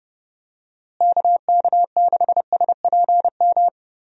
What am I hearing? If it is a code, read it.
Morse code, heard as a keyed tone, about 31 wpm, 700 Hz. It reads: KK6SPM
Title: USA Call Signs - 6 Characters - Repeated 2x - 30wpm